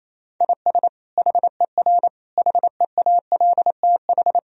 Morse code, heard as a keyed tone, about 28 wpm, 700 Hz. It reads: IS HER HEALTH
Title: Sentences from Top 1000 Words - Review Part 3 - 28wpm